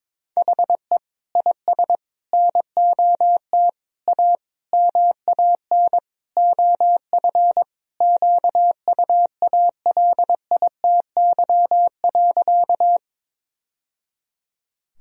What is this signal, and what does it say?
Morse code, heard as a keyed tone, about 21 wpm, 700 Hz. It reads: HE IS NOT A MAN OF QUALITY.
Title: Sentences from Top 1000 Words - Review Part 4 - 22wpm